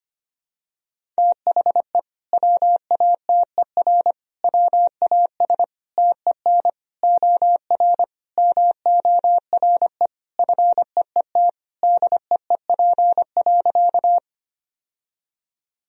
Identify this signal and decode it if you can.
Morse code, heard as a keyed tone, about 25 wpm, 700 Hz. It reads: THE WATER WAS TEN OR MORE FEET DEEP.
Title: Sentences from Top 1000 Words - Review Part 3 - 25wpm